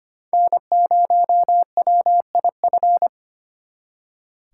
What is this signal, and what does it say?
Morse code, heard as a keyed tone, about 33 wpm, 700 Hz. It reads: N0WIF